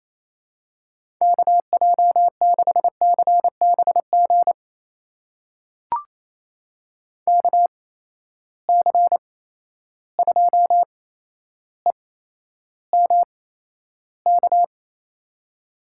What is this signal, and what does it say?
Morse code, heard as a keyed tone, about 28 wpm, spaced as for 12 wpm, 700 Hz. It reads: KJ6CBG KC2EMK
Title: USA Call Signs - 6 Characters - 8x Character Spacing - 28wpm